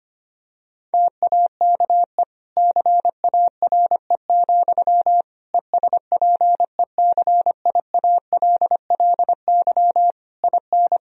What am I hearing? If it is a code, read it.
Morse code, heard as a keyed tone, about 25 wpm, 700 Hz. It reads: TAKE CARE, ESPECIALLY IN
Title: Sentences from Top 1000 Words - Review Part 4 - 25wpm